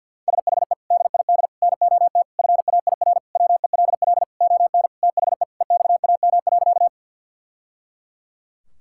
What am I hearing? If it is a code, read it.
Morse code, heard as a keyed tone, about 48 wpm, 690 Hz. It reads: SHE DID NOT FAIR WELL ON THE EXAM.
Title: Sentences from Top 1000 Words - Review Part 5 - 50wpm